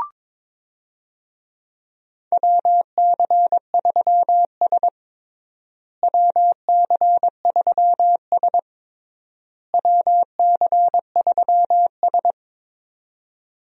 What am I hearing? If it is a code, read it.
Morse code, heard as a keyed tone, about 22 wpm, 700 Hz. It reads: WC3S WC3S WC3S
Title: USA Call Signs - 4 Characters - Repeated 3x - 22wpm